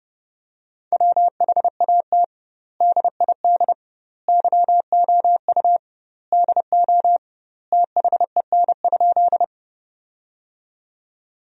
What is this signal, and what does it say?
Morse code, heard as a keyed tone, about 32 wpm, 700 Hz. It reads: WHAT DID YOU DO THEN?
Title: 2x Word-Spacing - Sentences from Top 100 Words - 30wpm